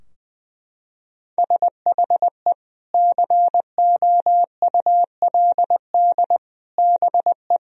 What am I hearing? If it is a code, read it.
Morse code, heard as a keyed tone, about 20 wpm, 700 Hz. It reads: SHE COULD BE
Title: Sentences from Top 1000 Words - Review Part 4 - 20wpm